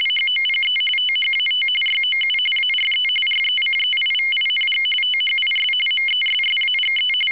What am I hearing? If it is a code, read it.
RTTY, about 45.45 baud. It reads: HOPE THAT YOU ARE RECEIVING THIS MESSAGE
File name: w ry850 / rtty45